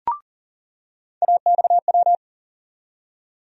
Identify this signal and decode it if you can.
Morse code, heard as a keyed tone, about 40 wpm, 700 Hz. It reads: AXW
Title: USA Call Sign Postfixes - 3 Letters 40wpm